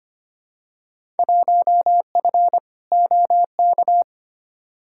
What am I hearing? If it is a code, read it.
Morse code, heard as a keyed tone, about 25 wpm, 700 Hz. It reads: 1F OK
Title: Field Day Exchanges Encore - 25wpm